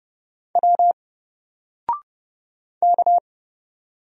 Morse code, heard as a keyed tone, about 31 wpm, 700 Hz. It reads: W K
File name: Single-Letter-Number-30wpm